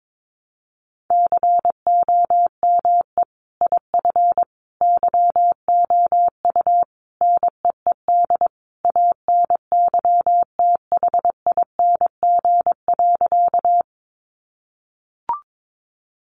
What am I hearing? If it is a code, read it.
Morse code, heard as a keyed tone, about 22 wpm, 700 Hz. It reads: COME IF YOU NEED ANYTHING.
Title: Sentences from Top 1000 Words 22wpm